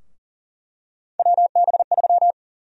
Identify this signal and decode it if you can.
Morse code, heard as a keyed tone, about 40 wpm, 700 Hz. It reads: WB3
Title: USA Call Sign Prefixes - Repeated 2x - 40wpm